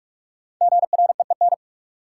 Morse code, heard as a keyed tone, about 42 wpm, 700 Hz. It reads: GREEN
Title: Sentences from Top 500 Words - New Word Drill - 45wpm